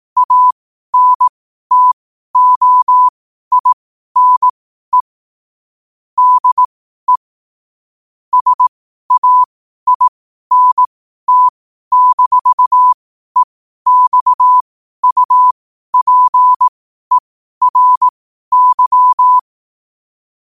Morse code, heard as a keyed tone, about 18 wpm, spaced as for 12 wpm, 1000 Hz. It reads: ANTOINE DE SAINT-EXUPERY
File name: qotdfiles / 12wpm